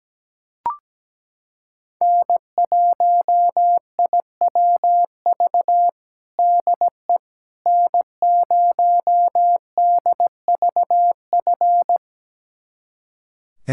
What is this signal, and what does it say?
Morse code, heard as a keyed tone, about 17 wpm, 700 Hz. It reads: N1IWV DE N0DVF
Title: USA Callsign DE Callsign - 5 Characters - 17wpm